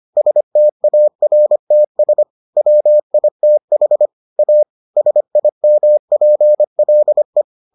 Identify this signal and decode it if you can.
Morse code, heard as a keyed tone, about 25 wpm, 600 Hz. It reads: STARTS WITH A SIMPLE